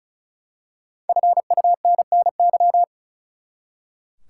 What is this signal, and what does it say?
Morse code, heard as a keyed tone, about 34 wpm, 700 Hz. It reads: FUNNY
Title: Five Letter Words - 35wpm